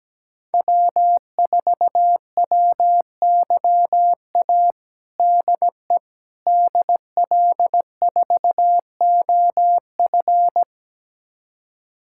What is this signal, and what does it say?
Morse code, heard as a keyed tone, about 17 wpm, 700 Hz. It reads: W4WYA DE DL4OF
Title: USA Callsign DE Callsign - 5 Characters - 17wpm